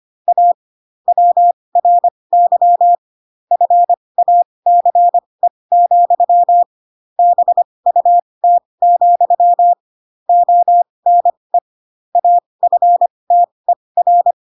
Morse code, heard as a keyed tone, about 25 wpm, spaced as for 20 wpm, 700 Hz. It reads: A WRY FACE, BUT, ONE AFTER
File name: Pinocchio / Pinocchio ch7 15 25 ik1hge